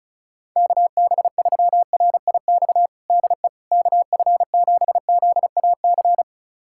Morse code, heard as a keyed tone, about 36 wpm, 700 Hz. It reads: KB3RIX DE KF7ZAC
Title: USA Callsign DE Callsign - 6 Characters - 35wpm